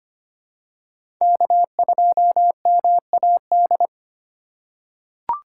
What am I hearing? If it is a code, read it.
Morse code, heard as a keyed tone, about 25 wpm, 700 Hz. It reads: K2MAD
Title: USA Call Signs - 5 Characters - 3x Character Spacing - 25wpm